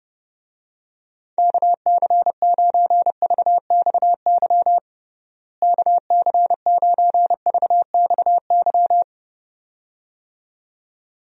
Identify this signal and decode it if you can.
Morse code, heard as a keyed tone, about 30 wpm, 700 Hz. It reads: KC9VXY KC9VXY
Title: USA Call Signs - 6 Characters - Repeated 2x - 30wpm